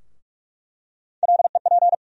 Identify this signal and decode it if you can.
Morse code, heard as a keyed tone, about 46 wpm, 700 Hz. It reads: LEP
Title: USA Call Sign Postfixes - 3 Letters 45wpm